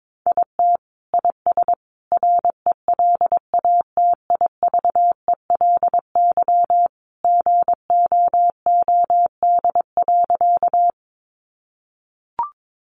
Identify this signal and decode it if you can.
Morse code, heard as a keyed tone, about 22 wpm, 700 Hz. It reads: IT IS RELATIVELY GOOD.
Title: Sentences from Top 1100 Words 22wpm